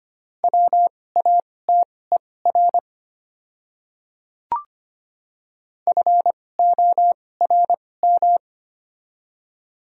Morse code, heard as a keyed tone, about 25 wpm, spaced as for 18 wpm, 700 Hz. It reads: WATER FORM
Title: 2x Character Spacing - Top 300 Words - 25wpm